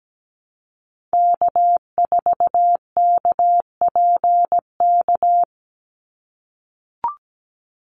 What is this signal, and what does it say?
Morse code, heard as a keyed tone, about 17 wpm, 700 Hz. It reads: K4KPK
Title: NA SOTA Activator Callsigns 17wpm